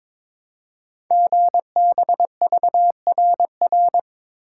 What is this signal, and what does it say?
Morse code, heard as a keyed tone, about 22 wpm, 700 Hz. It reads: GBVRR
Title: Code Groups - 5 Letters - 22wpm